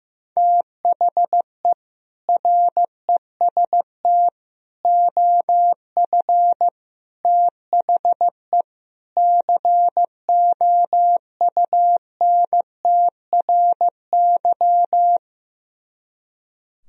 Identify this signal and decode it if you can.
Morse code, heard as a keyed tone, about 15 wpm, 700 Hz. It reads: THE REST OF THE COUNTRY
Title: Speed-Racing - Sets of 5 Words 15wpm